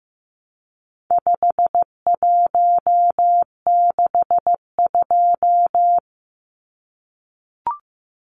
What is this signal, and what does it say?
Morse code, heard as a keyed tone, about 15 wpm, 700 Hz. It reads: 5162
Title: Numbers - 4-Digit 15wpm